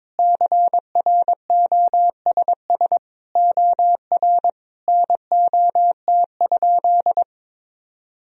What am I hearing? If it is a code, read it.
Morse code, heard as a keyed tone, about 22 wpm, 700 Hz. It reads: CROSS OR NOT?
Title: Sentences from Top 1000 Words - Review Part 3 - 22wpm